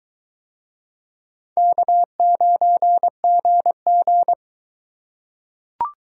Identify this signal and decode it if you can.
Morse code, heard as a keyed tone, about 22 wpm, 700 Hz. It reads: K9GG